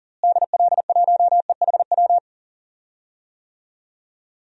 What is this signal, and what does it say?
Morse code, heard as a keyed tone, about 40 wpm, 700 Hz. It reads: DL1EHW
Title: DXCC-100 Call Signs - 6 Characters - Repeated 2x - 40wpm